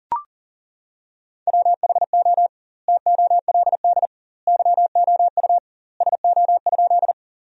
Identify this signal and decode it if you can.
Morse code, heard as a keyed tone, about 40 wpm, 700 Hz. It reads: WHO TOLD YOU SO?
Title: 2x Word-Spacing - Sentences from Top 400 Words - 40wpm